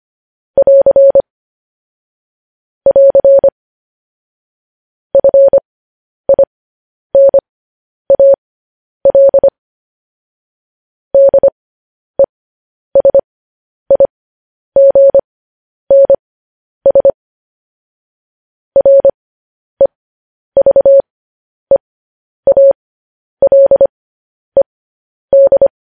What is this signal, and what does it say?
Morse code, heard as a keyed tone, about 25 wpm, spaced as for 10 wpm, 550 Hz. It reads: <AR> <AR> FINAL DESIGNS REVEALED
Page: news-headlines-in-morse-code-at-10-wpm